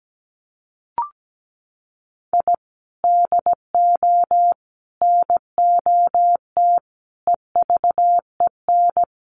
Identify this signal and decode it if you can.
Morse code, heard as a keyed tone, about 17 wpm, 700 Hz. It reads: I DO NOT EVEN
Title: Sets of 5 Words 17wpm